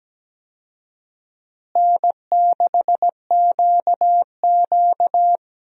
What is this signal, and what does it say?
Morse code, heard as a keyed tone, about 17 wpm, 700 Hz. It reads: N6QQ
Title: USA Call Signs - 4 Characters - Repeated 2x - 17wpm